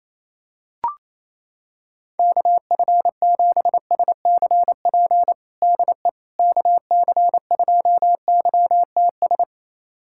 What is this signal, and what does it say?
Morse code, heard as a keyed tone, about 28 wpm, 700 Hz. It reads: KF7SCP DE KC2YTS
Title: USA Callsign DE Callsign - 6 Characters - 28wpm